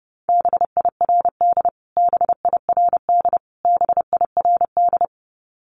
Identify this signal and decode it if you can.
Morse code, heard as a keyed tone, about 30 wpm, 700 Hz. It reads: BIRD BIRD BIRD
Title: Top 500 Words Repeated 30wpm